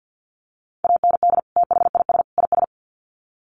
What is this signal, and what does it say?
Morse code, heard as a keyed tone, about 50 wpm, 700 Hz. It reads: AND THIS IS
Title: Sets of 3 Words 50wpm